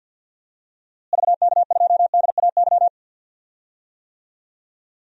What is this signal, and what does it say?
Morse code, heard as a keyed tone, about 51 wpm, 700 Hz. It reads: VK2DAY
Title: DXCC-100 Call Signs - 6 Characters - Repeated 3x - 50wpm